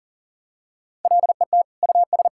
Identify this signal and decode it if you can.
Morse code, heard as a keyed tone, about 40 wpm, 700 Hz. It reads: LET US
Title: Sentences from Top 1000 Words - Review Part 6 - 40wpm